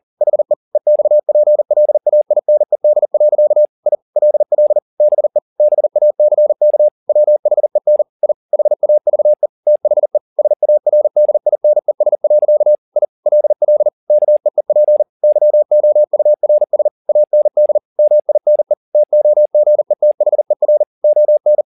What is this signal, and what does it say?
Morse code, heard as a keyed tone, about 40 wpm, 600 Hz. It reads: HE EXPLAINED. I LL BE BACK WHEN I HAVE THE SARDINES. I LL KEEP YOURS AND MINE TOGETHER ON